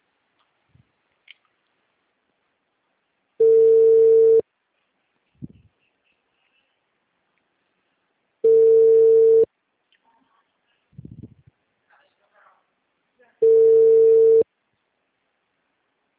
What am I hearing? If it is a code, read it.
Ringback tone (China, 450 Hz), with speech.